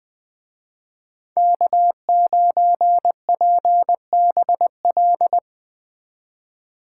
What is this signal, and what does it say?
Morse code, heard as a keyed tone, about 20 wpm, 700 Hz. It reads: K9PBL
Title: Speed Racing - USA 1x3 Call Signs 17wpm